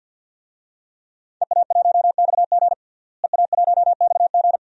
Morse code, heard as a keyed tone, about 51 wpm, 700 Hz. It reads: EA1XG EA1XG
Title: DXCC-100 Call Signs - 5 Characters - Repeated 2x - 50wpm